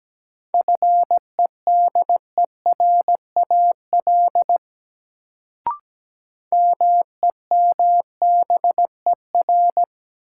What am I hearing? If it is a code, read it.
Morse code, heard as a keyed tone, about 17 wpm, 700 Hz. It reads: FEDERAL MEMBER
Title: Sentences from Top 1100 Words - New Word Drill Alt - 17wpm